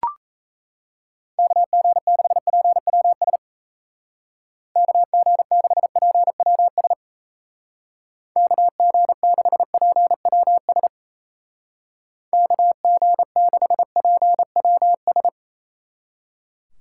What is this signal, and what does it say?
Morse code, heard as a keyed tone, about 35 wpm, 700 Hz. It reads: KG6PWS KG6PWS KG6PWS KG6PWS
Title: Speed Racing - USA 2x3 Call Signs 28wpm